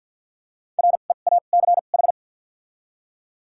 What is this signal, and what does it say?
Morse code, heard as a keyed tone, about 53 wpm, spaced as for 37 wpm, 700 Hz. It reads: REACH
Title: 2x Character Spacing - Top 500 Words - 50wpm